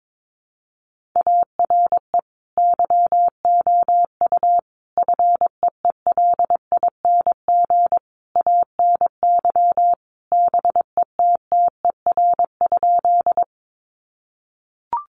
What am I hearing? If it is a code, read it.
Morse code, heard as a keyed tone, about 22 wpm, 700 Hz. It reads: ARE YOU FEELING ANY BETTER?
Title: Sentences from Top 1200 Words 22wpm